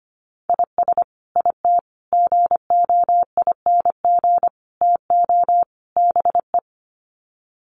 Morse code, heard as a keyed tone, about 25 wpm, 700 Hz. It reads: IS IT GOING TO BE
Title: Sets of 5 Words Encore2 25wpm